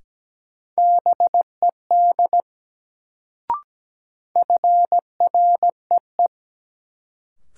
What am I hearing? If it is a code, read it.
Morse code, heard as a keyed tone, about 17 wpm, 700 Hz. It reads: BED FREE
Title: Sentences from Top 500 Words - New Word Drill - 17wpm